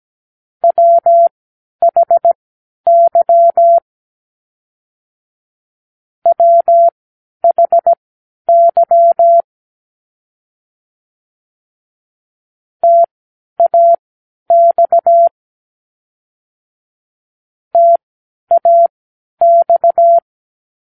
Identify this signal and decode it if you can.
Morse code, heard as a keyed tone, about 17 wpm, spaced as for 10 wpm, 680 Hz. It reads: WHY WHY TAX TAX